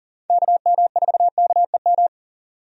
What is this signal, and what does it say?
Morse code, heard as a keyed tone, about 40 wpm, 700 Hz. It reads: KM4KEM
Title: USA Call Signs - 6 Characters - 40wpm